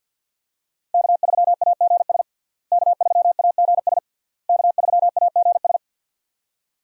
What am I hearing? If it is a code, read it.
Morse code, heard as a keyed tone, about 50 wpm, 700 Hz. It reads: K3AGS K3AGS K3AGS
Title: USA Call Signs - 5 Characters - Repeated 3x - 50wpm